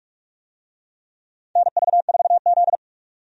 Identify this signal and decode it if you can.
Morse code, heard as a keyed tone, about 45 wpm, 700 Hz. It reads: NV4Z
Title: USA Call Signs - 4 Characters - 45wpm